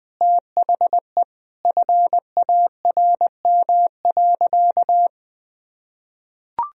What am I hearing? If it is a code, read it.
Morse code, heard as a keyed tone, about 21 wpm, 700 Hz. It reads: THE FARM.
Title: Sentences from Top 1000 Words - Review Part 2 - 20wpm